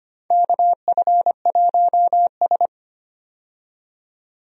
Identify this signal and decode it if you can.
Morse code, heard as a keyed tone, about 25 wpm, 700 Hz. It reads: KF1S
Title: USA Call Signs - 4 Characters - 25wpm